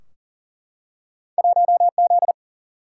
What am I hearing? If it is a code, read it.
Morse code, heard as a keyed tone, about 40 wpm, 700 Hz. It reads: 1 Z
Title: Single-Letter Single-Number 40wpm